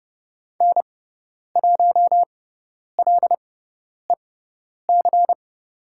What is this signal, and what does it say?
Morse code, heard as a keyed tone, about 30 wpm, spaced as for 10 wpm, 700 Hz. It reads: N1LEC
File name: Lesson 18g - Letter m - Call Signs 30wpm